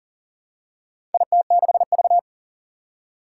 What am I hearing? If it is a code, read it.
Morse code, heard as a keyed tone, about 40 wpm, 690 Hz. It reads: IT6V
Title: Four Challenging Letter-Number Combinations 40wpm